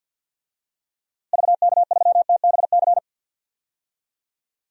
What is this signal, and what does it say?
Morse code, heard as a keyed tone, about 47 wpm, 700 Hz. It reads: VK3TBC
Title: DXCC-100 Call Signs - 6 Characters - Repeated 3x - 50wpm